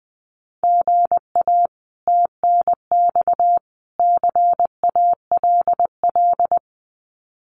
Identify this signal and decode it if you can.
Morse code, heard as a keyed tone, about 20 wpm, 700 Hz. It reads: GA TNX CALL
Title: Common QSO Phrases 20wpm